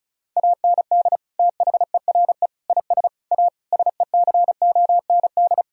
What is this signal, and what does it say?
Morse code, heard as a keyed tone, about 36 wpm, 700 Hz. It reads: AND THERE IS A SECOND
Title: Sentences from Top 1000 Words - Review Part 1 - 35wpm